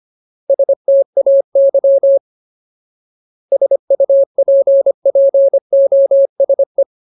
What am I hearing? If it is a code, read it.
Morse code, heard as a keyed tone, about 25 wpm, 550 Hz. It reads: STAY SUPPOSE
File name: Ultimate Word List / 25WPM Ultimate